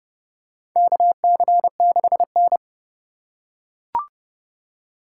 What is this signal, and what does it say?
Morse code, heard as a keyed tone, about 30 wpm, 700 Hz. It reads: KC6N
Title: USA Call Signs - 4 Characters - 3x Character Spacing - 30wpm